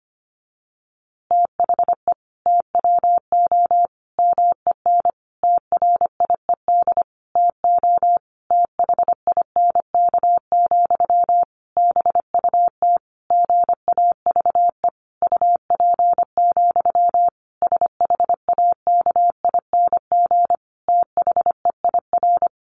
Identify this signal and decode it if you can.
Morse code, heard as a keyed tone, about 25 wpm, 700 Hz. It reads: THE TWO MEN TRIED TO THINK, BUT GAVE UP, SHAKING THEIR
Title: A Sound Of Thunder 25wpm